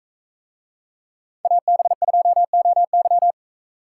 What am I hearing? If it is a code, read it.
Morse code, heard as a keyed tone, about 41 wpm, 700 Hz. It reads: AB2OY